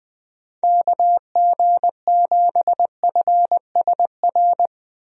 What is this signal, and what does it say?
Morse code, heard as a keyed tone, about 21 wpm, 700 Hz. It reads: KG7FSR
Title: USA Call Signs - 6 Characters - Repeated 2x - 20wpm